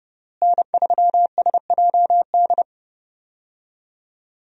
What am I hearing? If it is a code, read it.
Morse code, heard as a keyed tone, about 30 wpm, 700 Hz. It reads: N3SJD
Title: USA Call Signs - 5 Characters - Repeated 3x - 30wpm